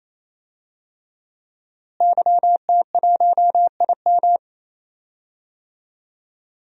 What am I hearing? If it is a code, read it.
Morse code, heard as a keyed tone, about 28 wpm, 700 Hz. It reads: YT1IM
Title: DXCC-100 Call Signs - 5 Characters - Repeated 3x - 28wpm